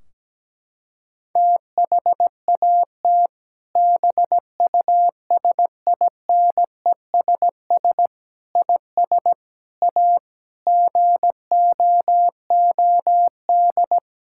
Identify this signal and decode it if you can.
Morse code, heard as a keyed tone, about 17 wpm, 700 Hz. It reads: THAT BUSINESS IS A GOOD
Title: Sentences from Top 1600 Words 17wpm